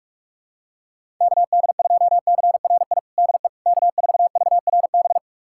Morse code, heard as a keyed tone, about 41 wpm, 700 Hz. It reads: KD2CRI DE K4VRB
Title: USA Callsign DE Callsign 45wpm